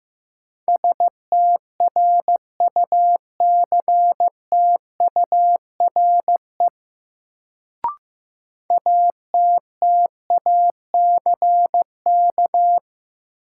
Morse code, heard as a keyed tone, about 15 wpm, 700 Hz. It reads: STRUCTURE ATTACK
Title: Sentences from Top 1500 Words - New Word Drill Alt - 15wpm